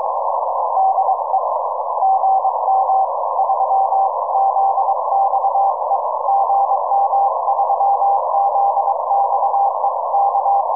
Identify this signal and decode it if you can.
Morse code, heard as a keyed tone, about 23 wpm, 790 Hz, with noise. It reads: DE GM3ZXE GM3ZXE K
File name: iw0ffk / 2006 06 07 gm3zxe iw0ffk